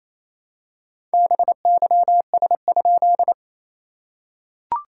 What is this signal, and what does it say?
Morse code, heard as a keyed tone, about 27 wpm, 690 Hz. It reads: BYS?
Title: SOTA QSO Practice 28wpm